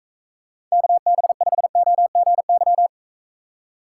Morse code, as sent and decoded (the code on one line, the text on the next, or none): -.- -... ..... --- --. -.--
KB5OGY